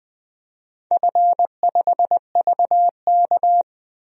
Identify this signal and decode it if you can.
Morse code, heard as a keyed tone, about 20 wpm, 700 Hz. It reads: F5VK
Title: DXCC-100 Call Signs - 4 Characters - 20wpm